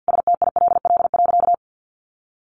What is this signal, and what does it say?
Morse code, heard as a keyed tone, about 50 wpm, 700 Hz. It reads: STILL.
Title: Sentences from Top 500 Words 50wpm